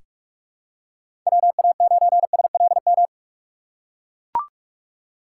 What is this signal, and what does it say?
Morse code, heard as a keyed tone, about 44 wpm, 700 Hz. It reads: WA9SLM
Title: USA Call Signs - 6 Characters - 6x Character Spacing - 45wpm